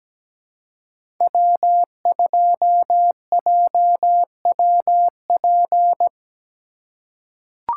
Morse code, heard as a keyed tone, about 17 wpm, 700 Hz. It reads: W2JWP
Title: USA Call Signs - 5 Characters - 4x Character Spacing - 17wpm